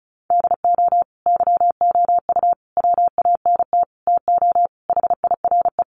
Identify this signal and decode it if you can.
Morse code, heard as a keyed tone, about 34 wpm, 700 Hz. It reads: DO YOU WANT TO HIRE
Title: Sentences from Top 1400 Words 35wpm